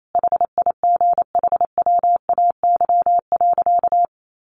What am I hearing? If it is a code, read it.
Morse code, heard as a keyed tone, about 28 wpm, 700 Hz. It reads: HIGHWAY.
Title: Sentences from Top 1400 Words 28wpm